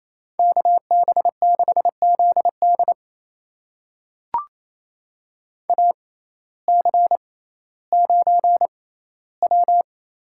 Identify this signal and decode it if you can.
Morse code, heard as a keyed tone, about 27 wpm, spaced as for 15 wpm, 700 Hz. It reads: KB6ZD AC9W
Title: USA Call Signs - 6x Character Spacing - 28wpm